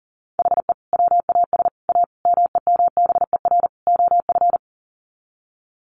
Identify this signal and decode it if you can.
Morse code, heard as a keyed tone, about 40 wpm, 700 Hz. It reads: HE WAS A MEMBER OF
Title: Sets of 5 Words Encore2 40wpm